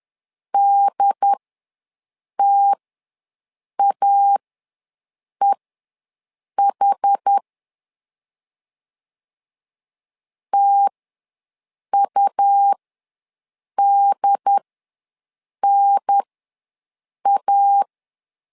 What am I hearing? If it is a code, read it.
Morse code, heard as a keyed tone, about 10 wpm, spaced as for 6 wpm, 800 Hz. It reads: DTAEH TUDNA
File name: 2 UHDN ws 10wpm